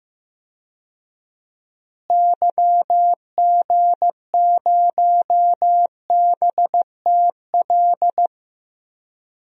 Morse code, heard as a keyed tone, about 15 wpm, 700 Hz. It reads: YG0BTL
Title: DXCC-100 Call Signs - 6 Characters - Repeated 2x - 15wpm